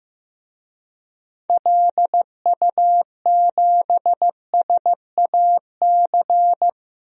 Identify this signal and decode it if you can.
Morse code, heard as a keyed tone, about 15 wpm, 700 Hz. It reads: LU7SAC
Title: DXCC-100 Call Signs - 6 Characters - Repeated 3x - 15wpm